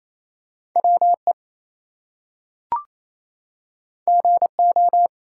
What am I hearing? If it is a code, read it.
Morse code, heard as a keyed tone, about 28 wpm, 700 Hz. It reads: WE GO
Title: Two Letter Words - 28wpm